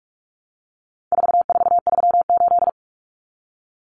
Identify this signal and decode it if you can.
Morse code, heard as a keyed tone, about 44 wpm, 700 Hz. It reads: 4438
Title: Numbers - 4-Digit 45wpm